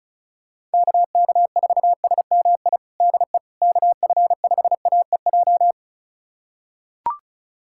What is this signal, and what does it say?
Morse code, heard as a keyed tone, about 36 wpm, 700 Hz. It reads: KK4SMI DE KF5AEJ
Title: USA Callsign DE Callsign - 6 Characters - 35wpm